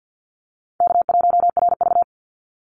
Morse code, heard as a keyed tone, about 50 wpm, 700 Hz. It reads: K1RV
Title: Easy Call Signs - 50wpm